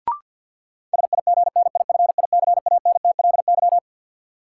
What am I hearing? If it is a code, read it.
Morse code, heard as a keyed tone, about 53 wpm, 700 Hz. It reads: SIGNIFICANTLY
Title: Sentences from Top 1500 Words - New Word Drill Alt - 50wpm